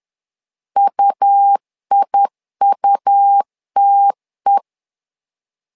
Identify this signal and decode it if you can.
Morse code, heard as a keyed tone, about 10 wpm, 800 Hz. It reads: UIUTE
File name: LEZIONI / 2 UHDN / 2 U 10wpm 1622